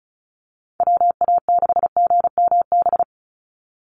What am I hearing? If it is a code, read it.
Morse code, heard as a keyed tone, about 34 wpm, 700 Hz. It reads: WA6GMB
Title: USA Call Signs 35wpm